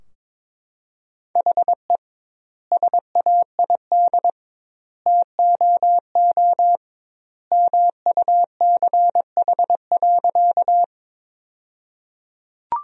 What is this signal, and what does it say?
Morse code, heard as a keyed tone, about 22 wpm, 700 Hz. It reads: HE SAID TOO MUCH.
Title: 2x Word-Spacing - Sentences from Top 200 Words - 22wpm